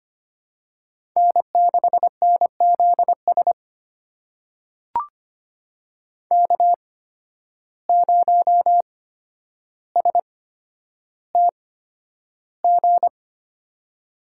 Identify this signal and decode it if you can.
Morse code, heard as a keyed tone, about 25 wpm, spaced as for 11 wpm, 700 Hz. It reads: N6NZS K0STG
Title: USA Call Signs - 5 Characters - 8x Character Spacing - 25wpm